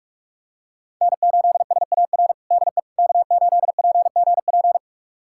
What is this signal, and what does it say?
Morse code, heard as a keyed tone, about 46 wpm, 700 Hz. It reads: N8SAR DE K8PGP
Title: USA Callsign DE Callsign - 5 Characters - 45wpm